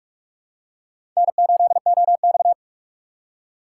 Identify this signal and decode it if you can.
Morse code, heard as a keyed tone, about 46 wpm, 700 Hz. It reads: N8OX
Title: USA Call Signs - 4 Characters - 4x Character Spacing - 45wpm